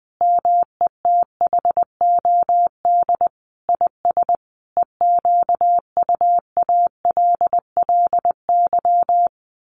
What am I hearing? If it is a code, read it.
Morse code, heard as a keyed tone, about 19 wpm, 700 Hz. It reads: METHOD IS EQUALLY
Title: Sentences from Top 1400 Words 20wpm